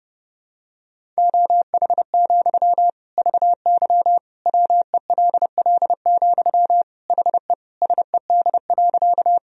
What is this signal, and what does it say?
Morse code, heard as a keyed tone, about 30 wpm, 700 Hz. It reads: OH, VY WELL, HE SED.
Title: Sentences from Top 500 Words and Abbreviations 30wpm